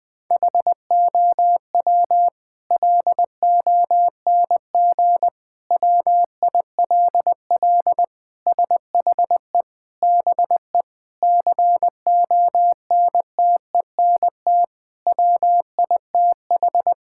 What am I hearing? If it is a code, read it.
Morse code, heard as a keyed tone, about 21 wpm, 700 Hz. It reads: HOW LONG WILL SHE BE CONTENT WITH
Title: Sentences from Top 1000 Words - Review Part 6 - 20wpm